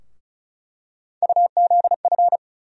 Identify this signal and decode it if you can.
Morse code, heard as a keyed tone, about 36 wpm, 700 Hz. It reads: UZF
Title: USA Call Sign Postfixes - 3 Letters 35wpm